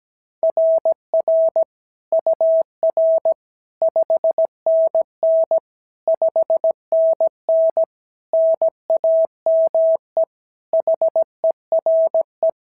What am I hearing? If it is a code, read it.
Morse code, heard as a keyed tone, about 17 wpm, 650 Hz. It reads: RR UR 5NN 5NN NAME HERE
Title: SOTA QSO Practice 17wpm